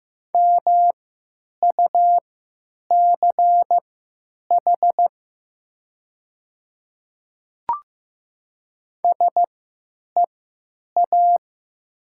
Morse code, heard as a keyed tone, about 15 wpm, spaced as for 9 wpm, 700 Hz. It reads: MUCH SEA
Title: 3x Character Spacing - Top 300 Words - 15wpm